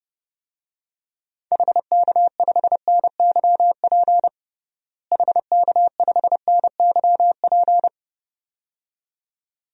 Morse code, heard as a keyed tone, about 30 wpm, 700 Hz. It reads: HK5NYP HK5NYP
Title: DXCC-100 Call Signs - 6 Characters - Repeated 2x - 30wpm